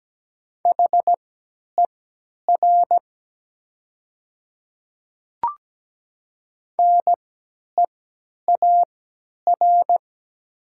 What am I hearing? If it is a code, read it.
Morse code, heard as a keyed tone, about 17 wpm, spaced as for 10 wpm, 700 Hz. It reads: HER NEAR
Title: 3x Character Spacing - Top 200 Words - 17wpm